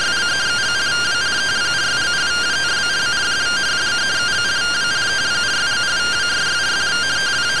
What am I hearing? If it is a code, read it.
Two-tone teleprinter signal, about 170 Hz shift.